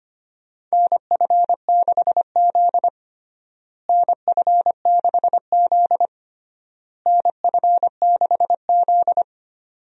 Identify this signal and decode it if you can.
Morse code, heard as a keyed tone, about 25 wpm, 700 Hz. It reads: NF6Z NF6Z NF6Z
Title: USA Call Signs - 4 Characters - Repeated 3x - 25wpm